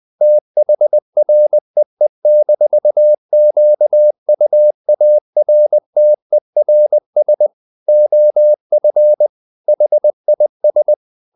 Morse code, heard as a keyed tone, about 20 wpm, 600 Hz. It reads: THREE-QUARTERS OF HIS
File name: the old man and the sea part 2 20wpm